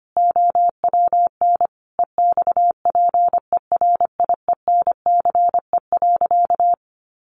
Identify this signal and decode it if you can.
Morse code, heard as a keyed tone, about 25 wpm, 700 Hz. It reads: OWN EXPERIENCE.
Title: Sentences from Top 600 Words 25wpm